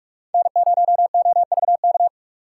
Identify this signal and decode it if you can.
Morse code, heard as a keyed tone, about 46 wpm, 700 Hz. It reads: N0OVK
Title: USA Call Signs - 3x Character Spacing - 45wpm